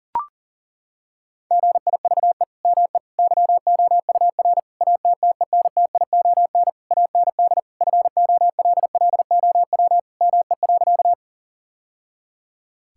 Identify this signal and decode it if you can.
Morse code, heard as a keyed tone, about 40 wpm, 700 Hz. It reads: GIVE ME YOUR ATTENTION AND FOLLOW ME.
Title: Sentences from Top 1000 Words - Review Part 6 - 40wpm